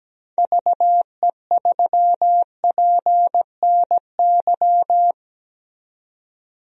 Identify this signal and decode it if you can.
Morse code, heard as a keyed tone, about 17 wpm, 700 Hz. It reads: VE3PNY
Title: DXCC-100 Call Signs - 6 Characters - 17wpm